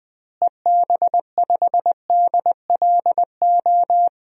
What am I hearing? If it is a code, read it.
Morse code, heard as a keyed tone, about 21 wpm, 700 Hz. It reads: EB5DLO
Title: DXCC-100 Call Signs - 6 Characters - Repeated 2x - 20wpm